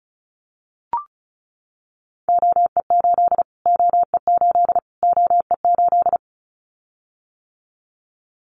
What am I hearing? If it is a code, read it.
Morse code, heard as a keyed tone, about 34 wpm, 700 Hz. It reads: OE8 OE8 OE8
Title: Callsign Prefixes for Top 25 DXCC - Repeated 3x - 35wpm